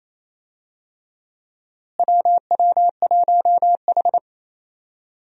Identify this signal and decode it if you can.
Morse code, heard as a keyed tone, about 28 wpm, 700 Hz. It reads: WW1H
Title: USA Call Signs - 4 Characters - Repeated 3x - 28wpm